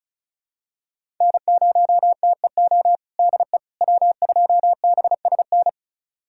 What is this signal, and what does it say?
Morse code, heard as a keyed tone, about 36 wpm, 700 Hz. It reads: N0TEO DE W2BSN
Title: USA Callsign DE Callsign - 5 Characters - 35wpm